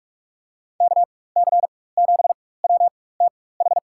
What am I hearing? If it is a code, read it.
Morse code, heard as a keyed tone, about 47 wpm, spaced as for 21 wpm, 700 Hz. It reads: KC7WTH
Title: USA Call Signs - 6 Characters - 4x Character Spacing - 45wpm